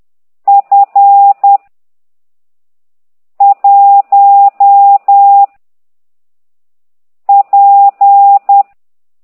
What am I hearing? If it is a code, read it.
Morse code, heard as a keyed tone, about 10 wpm, 800 Hz. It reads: F 1 P